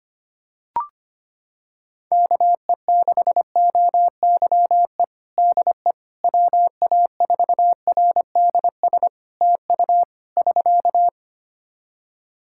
Morse code, heard as a keyed tone, about 25 wpm, 700 Hz. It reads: KE6OYE DE WA4RDS TU <SK>
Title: Practice QSO - Exchange 4 - Ending - 25wpm